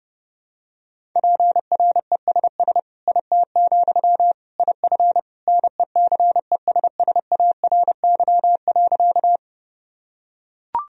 Morse code, heard as a keyed tone, about 29 wpm, 700 Hz. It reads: PRESS IT, IF NECESSARY.
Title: Sentences from Top 1000 Words - Review Part 1 - 30wpm